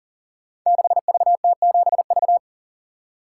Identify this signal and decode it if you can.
Morse code, heard as a keyed tone, about 42 wpm, 690 Hz. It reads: 6VT7V
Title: Code Groups - 5 Letter:Numbers - 40wpm